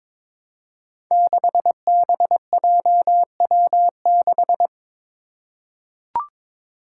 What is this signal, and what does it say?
Morse code, heard as a keyed tone, about 22 wpm, 700 Hz. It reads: 6BJW6
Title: Five Challenging Letter-Number Combinations 22wpm